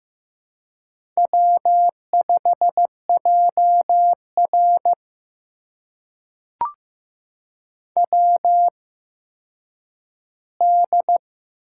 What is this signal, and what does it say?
Morse code, heard as a keyed tone, about 15 wpm, spaced as for 9 wpm, 700 Hz. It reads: W5JR WD
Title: USA Call Signs - 4 Characters - 8x Character Spacing - 15wpm